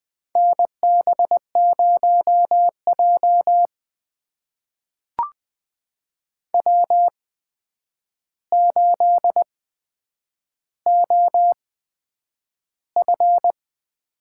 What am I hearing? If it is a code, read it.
Morse code, heard as a keyed tone, about 20 wpm, spaced as for 9 wpm, 700 Hz. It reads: NB0J W8OF
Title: USA Call Signs - 4 Characters - 8x Character Spacing - 20wpm